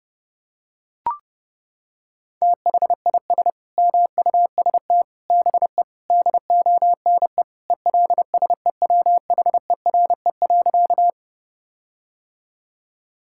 Morse code, heard as a keyed tone, about 32 wpm, 700 Hz. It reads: THIS MUST BE DONE ELSEWHERE.
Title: Sentences from Top 1600 Words 30wpm